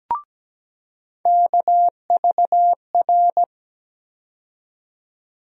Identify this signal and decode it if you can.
Morse code, heard as a keyed tone, about 17 wpm, 700 Hz. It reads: KVR
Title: USA Call Sign Postfixes - 3 Letters 17wpm